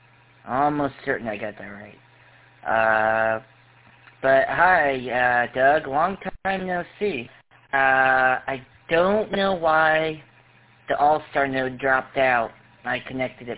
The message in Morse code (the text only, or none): none